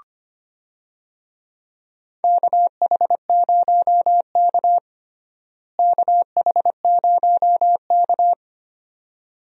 Morse code, heard as a keyed tone, about 25 wpm, 700 Hz. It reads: KH0K KH0K
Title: USA Call Signs - 4 Characters - Repeated 2x - 25wpm